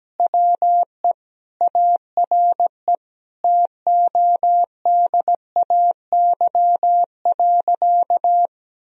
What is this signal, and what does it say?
Morse code, heard as a keyed tone, about 17 wpm, 700 Hz. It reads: WE ARE TODAY.
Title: Sentences from Top 1000 Words - Review Part 4 - 17wpm